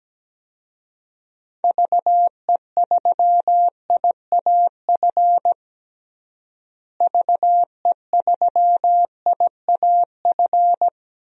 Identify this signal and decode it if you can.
Morse code, heard as a keyed tone, about 17 wpm, 700 Hz. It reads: VE3IAF VE3IAF
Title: DXCC-100 Call Signs - 6 Characters - Repeated 2x - 17wpm